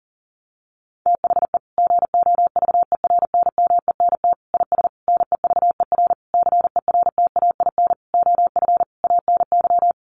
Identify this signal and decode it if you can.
Morse code, heard as a keyed tone, about 40 wpm, 700 Hz. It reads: THE GOVERNMENT IS NEVER CERTAIN OF ANY
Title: Sentences from Top 1300 Words 40wpm